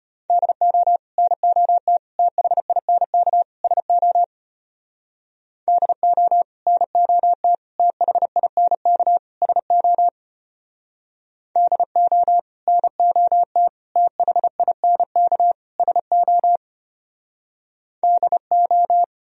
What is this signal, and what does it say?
Morse code, heard as a keyed tone, about 33 wpm, 700 Hz. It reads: DO NOT THINK SO DO NOT THINK SO DO NOT THINK SO DO